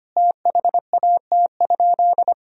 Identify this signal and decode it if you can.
Morse code, heard as a keyed tone, about 25 wpm, 700 Hz. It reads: THAT?
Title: Sentences from Top 100 Words and Abbreviations 25wpm